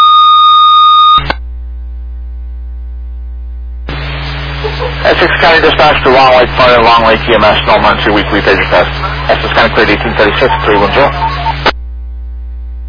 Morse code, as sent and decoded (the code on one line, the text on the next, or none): none